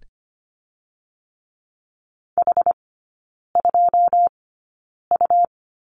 Morse code, heard as a keyed tone, about 25 wpm, spaced as for 9 wpm, 700 Hz. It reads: H2U
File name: course / 25:10wpm / 100 Lesson 17e - Letter u - Three Characters 25wpm